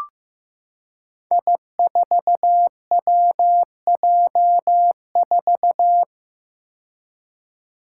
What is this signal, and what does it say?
Morse code, heard as a keyed tone, about 15 wpm, 700 Hz. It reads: I4WJ4
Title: Five Challenging Letter-Number Combinations 15wpm